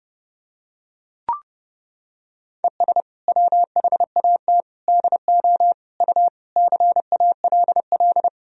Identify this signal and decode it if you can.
Morse code, heard as a keyed tone, about 30 wpm, 700 Hz. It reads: ES WHAT DO U CALL
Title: Sentences from Top 100 Words and Abbreviations 30wpm